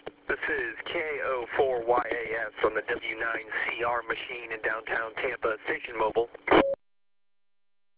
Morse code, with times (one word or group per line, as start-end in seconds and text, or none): none